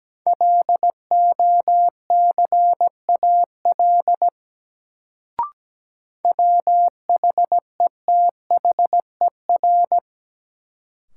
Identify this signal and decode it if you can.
Morse code, heard as a keyed tone, about 17 wpm, 700 Hz. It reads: LOCAL WHETHER
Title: Sentences from Top 1000 Words - Review Part 3 - New Word Drill - 17wpm